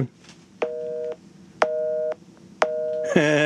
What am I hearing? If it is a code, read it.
Busy tone (North America), 480 + 620 Hz, with speech.